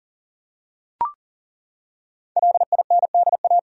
Morse code, heard as a keyed tone, about 40 wpm, 700 Hz. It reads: LINDA
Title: USA Common Names - Max 5 Letters - 40wpm